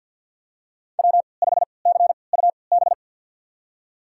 Morse code, heard as a keyed tone, about 53 wpm, spaced as for 29 wpm, 700 Hz. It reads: W5CUB
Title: USA Call Signs - 5 Characters - 3x Character Spacing - 50wpm